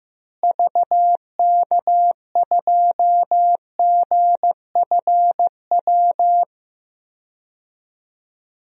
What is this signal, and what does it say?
Morse code, heard as a keyed tone, about 15 wpm, 700 Hz. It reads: VK2GFW
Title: DXCC-100 Call Signs - 6 Characters - Repeated 2x - 15wpm